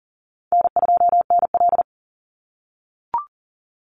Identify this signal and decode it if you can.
Morse code, heard as a keyed tone, about 42 wpm, 700 Hz. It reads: N2NL
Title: Easy Call Signs - 40wpm